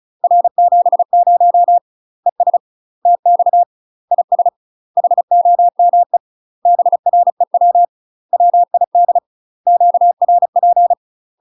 Morse code, heard as a keyed tone, about 36 wpm, 700 Hz. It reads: R70 ES TX IS HOME BREW WID QRP
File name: UK MASTER QSOs TRAINING / 35WPM / UK MASTER QSOs 0026